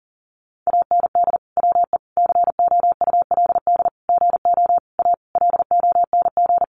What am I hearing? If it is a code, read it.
Morse code, heard as a keyed tone, about 40 wpm, 700 Hz. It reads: AND WE COULD GO A LONG
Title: Sentences from Top 1200 Words 40wpm